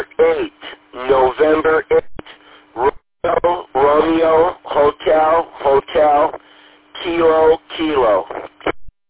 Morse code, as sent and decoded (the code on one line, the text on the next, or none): none